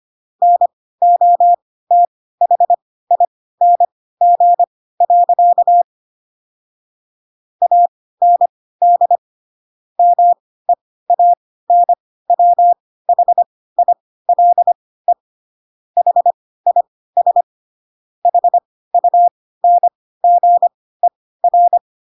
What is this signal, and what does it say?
Morse code, heard as a keyed tone, about 25 wpm, spaced as for 16 wpm, 700 Hz. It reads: NOTHING. AND MEANWHILE HIS HUNGER
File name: Pinocchio / Pinocchio ch5 10 25 ik1hge